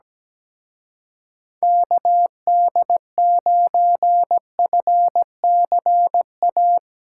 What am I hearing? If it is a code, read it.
Morse code, heard as a keyed tone, about 18 wpm, 700 Hz. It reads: KD9FCA
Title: Speed Racing - USA Call Signs 17wpm